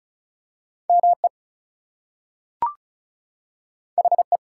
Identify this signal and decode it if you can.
Morse code, heard as a keyed tone, about 36 wpm, 700 Hz. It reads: ME HE
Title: Two Letter Words - 35wpm